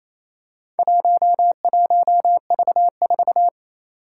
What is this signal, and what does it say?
Morse code, heard as a keyed tone, about 28 wpm, 700 Hz. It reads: 11V4
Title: Four Challenging Letter-Number Combinations 28wpm